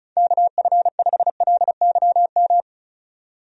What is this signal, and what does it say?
Morse code, heard as a keyed tone, about 36 wpm, 700 Hz. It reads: KF5LYM
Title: USA Call Signs - 6 Characters - 35wpm